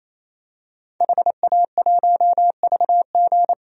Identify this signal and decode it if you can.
Morse code, heard as a keyed tone, about 28 wpm, 700 Hz. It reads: HA1VG
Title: DXCC-100 Call Signs - 5 Characters - 28wpm